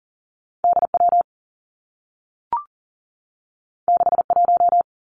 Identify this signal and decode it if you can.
Morse code, heard as a keyed tone, about 40 wpm, 700 Hz. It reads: D W 6 1
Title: Two Challenging Letter-Number Combinations 40wpm